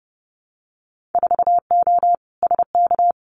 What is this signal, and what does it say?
Morse code, heard as a keyed tone, about 30 wpm, 700 Hz. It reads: 4O SK
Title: Winter Field Day Exchanges Encore 30wpm